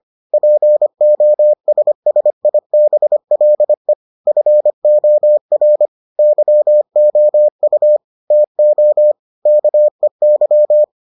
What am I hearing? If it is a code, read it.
Morse code, heard as a keyed tone, about 25 wpm, 600 Hz. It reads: POSSIBLE FOR YOU TO KEY